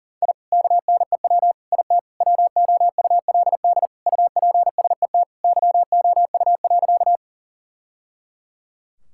Morse code, heard as a keyed tone, about 40 wpm, 700 Hz. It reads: I KNEW IT WOULD UPSET YOU.
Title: Sentences from Top 1000 Words - Review Part 5 - 40wpm